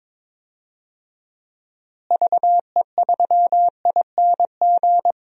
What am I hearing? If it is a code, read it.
Morse code, heard as a keyed tone, about 22 wpm, 700 Hz. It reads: VE3ING
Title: DXCC-100 Call Signs - 6 Characters - Repeated 3x - 22wpm